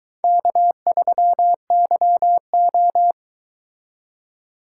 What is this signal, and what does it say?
Morse code, heard as a keyed tone, about 22 wpm, 700 Hz. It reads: K3YO